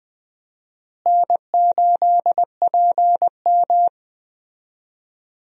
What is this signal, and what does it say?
Morse code, heard as a keyed tone, about 20 wpm, 700 Hz. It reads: N8PM
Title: USA Call Signs - 4 Characters - Repeated 2x - 20wpm